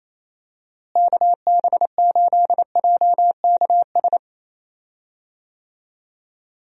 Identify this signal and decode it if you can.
Morse code, heard as a keyed tone, about 28 wpm, 700 Hz. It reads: KB8JKS
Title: USA Call Signs - 6 Characters - Repeated 2x - 28wpm